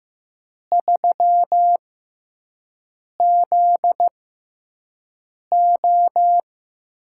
Morse code, heard as a keyed tone, about 16 wpm, spaced as for 5 wpm, 700 Hz. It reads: 3ZO
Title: USA Call Signs - 6 Characters - 6x Character Spacing - 15wpm